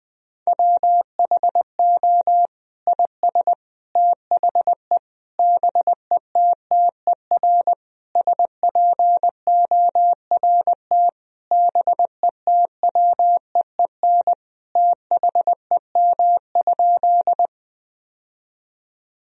Morse code, heard as a keyed tone, about 20 wpm, 700 Hz. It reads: WHO IS THE BETTER SPORT BETWEEN THEM?
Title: Sentences from Top 1000 Words - Review Part 4 - 20wpm